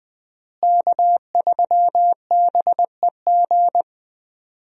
Morse code, heard as a keyed tone, about 20 wpm, 700 Hz. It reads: K3BEG
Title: USA Call Signs - 5 Characters - Repeated 2x - 20wpm